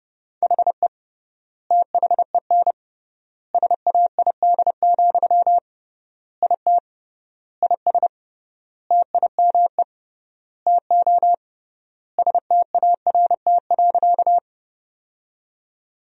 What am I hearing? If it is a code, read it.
Morse code, heard as a keyed tone, about 32 wpm, 700 Hz. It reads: HE THEN SAID, IT IS TIME TO START.
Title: 3x Word-Spacing - Sentences from Top 300 Words - 30wpm